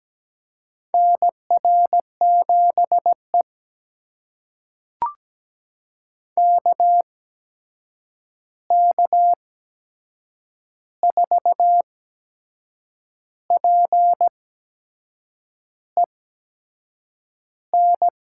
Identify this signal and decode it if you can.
Morse code, heard as a keyed tone, about 17 wpm, spaced as for 6 wpm, 700 Hz. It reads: NR7E KK4PEN
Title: USA Call Signs - 8x Character Spacing - 17wpm